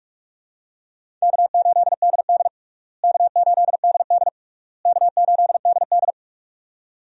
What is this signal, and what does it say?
Morse code, heard as a keyed tone, about 46 wpm, 700 Hz. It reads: K8DD K8DD K8DD
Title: USA Call Signs - 4 Characters - Repeated 3x - 45wpm